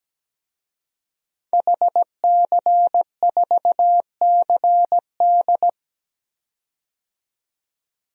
Morse code, heard as a keyed tone, about 17 wpm, 700 Hz. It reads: HC4CD
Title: DXCC-100 Call Signs - 5 Characters - 17wpm